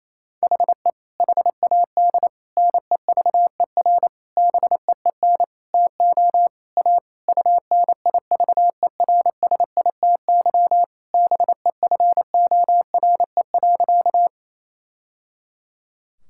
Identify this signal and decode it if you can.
Morse code, heard as a keyed tone, about 28 wpm, 700 Hz. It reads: HE HAD NEVER BEEN TO A UNIVERSITY BEFORE.
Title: Sentences from Top 1000 Words - Review Part 5 - 28wpm